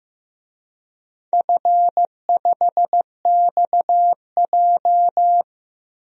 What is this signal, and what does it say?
Morse code, heard as a keyed tone, about 16 wpm, 700 Hz. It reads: F5XJ
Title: DXCC-100 Call Signs - 4 Characters - Repeated 2x - 15wpm